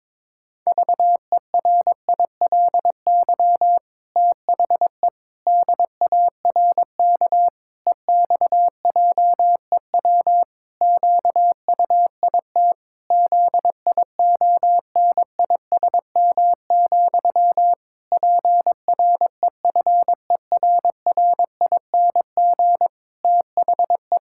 Morse code, heard as a keyed tone, about 22 wpm, 700 Hz. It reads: VERILY THE DARK EXJEW QUIT ZIONISM, PREFERRING THE
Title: Pangram Challenge v2 - 22wpm